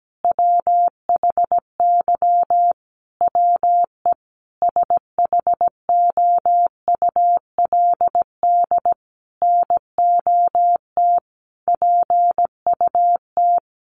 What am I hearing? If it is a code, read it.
Morse code, heard as a keyed tone, about 17 wpm, 700 Hz. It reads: WHY WE SHOULD NOT PUT
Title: Sentences from Top 1100 Words 17wpm